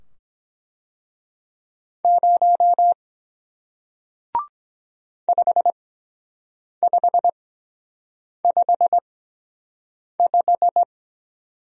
Morse code, heard as a keyed tone, about 22 wpm, spaced as for 7 wpm, 700 Hz. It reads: 0 5555